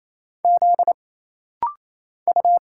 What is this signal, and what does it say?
Morse code, heard as a keyed tone, about 27 wpm, 700 Hz. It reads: Z U
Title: Single Letters - Rapid-Fire 28wpm